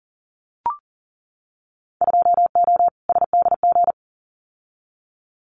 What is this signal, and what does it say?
Morse code, heard as a keyed tone, about 40 wpm, 700 Hz. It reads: 2O SDG
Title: Winter Field Day Exchanges 40wpm